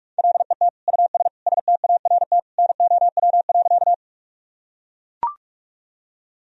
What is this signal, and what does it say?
Morse code, heard as a keyed tone, about 41 wpm, 700 Hz. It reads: LET US START NOW.
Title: Sentences from Top 1000 Words - Review Part 1 - 45wpm